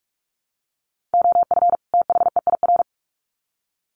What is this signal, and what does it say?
Morse code, heard as a keyed tone, about 44 wpm, 700 Hz. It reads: OF THEIR
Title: Sets of 2 Words 45wpm